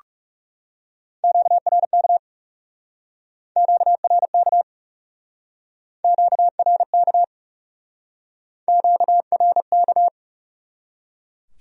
Morse code, heard as a keyed tone, about 38 wpm, 700 Hz. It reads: QRK QRK QRK QRK
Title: Speed-Racing Q-Codes - 30wpm